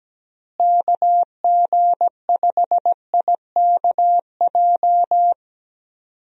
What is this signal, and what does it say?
Morse code, heard as a keyed tone, about 17 wpm, 700 Hz. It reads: KG5IKJ
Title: USA Call Signs - 6 Characters - Repeated 2x - 17wpm